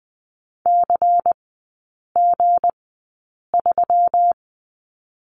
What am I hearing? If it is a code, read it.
Morse code, heard as a keyed tone, about 20 wpm, spaced as for 8 wpm, 700 Hz. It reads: CG3
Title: 178 Lesson 29e - Slash - Three Characters 20wpm